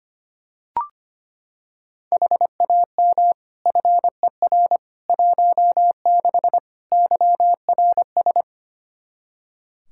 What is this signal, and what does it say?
Morse code, heard as a keyed tone, about 25 wpm, 700 Hz. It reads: HAM FER 16 YRS
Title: K1USN SST QSO Practice 25wpm